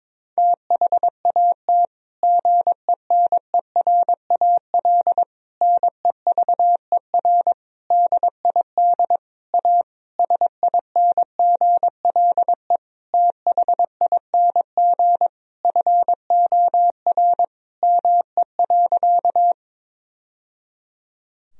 Morse code, heard as a keyed tone, about 22 wpm, 700 Hz. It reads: THAT GENERAL NEVER DID A SINGLE THING FOR ME.
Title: Sentences from Top 1000 Words - Review Part 4 - 22wpm